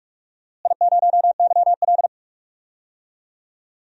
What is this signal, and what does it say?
Morse code, heard as a keyed tone, about 46 wpm, 700 Hz. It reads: I0YL
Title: DXCC-100 Call Signs - 4 Characters - Repeated 3x - 45wpm